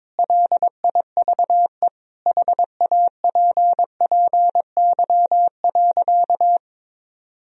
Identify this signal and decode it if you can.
Morse code, heard as a keyed tone, about 22 wpm, 700 Hz. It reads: LIVE HAPPY.
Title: Sentences from Top 1000 Words - Review Part 3 - 22wpm